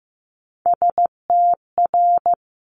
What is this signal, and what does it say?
Morse code, heard as a keyed tone, about 15 wpm, 700 Hz. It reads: STR
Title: Multi-letter Phonemes 15wpm